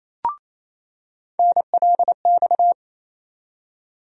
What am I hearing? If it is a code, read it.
Morse code, heard as a keyed tone, about 28 wpm, 700 Hz. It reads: NLX
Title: USA Call Sign Postfixes - 3 Letters 28wpm